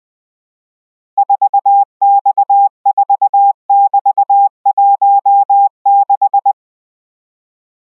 Morse code, heard as a keyed tone, about 20 wpm, 800 Hz. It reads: 4X4<BT>16